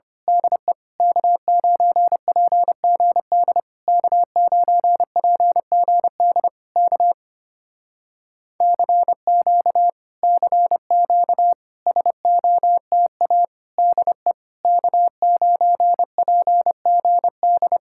Morse code, heard as a keyed tone, about 25 wpm, 700 Hz. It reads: DE K9PGD K9PGD K CQ CQ SOTA DE K9PGD
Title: Speed-Racing Common QSO Phrases Encore - 25wpm